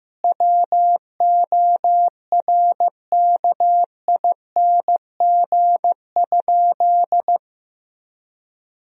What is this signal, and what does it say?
Morse code, heard as a keyed tone, about 15 wpm, 700 Hz. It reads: WORKING?
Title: Sentences from Top 1000 Words - Review Part 4 - 15wpm